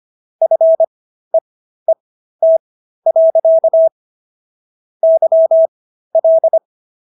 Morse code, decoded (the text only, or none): FEET. YL